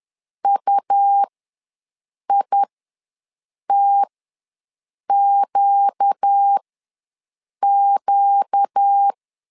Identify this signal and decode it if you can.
Morse code, heard as a keyed tone, about 10 wpm, spaced as for 6 wpm, 800 Hz. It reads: UITQQ
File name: LEZIONI / 4 Q ws 10wpm 2306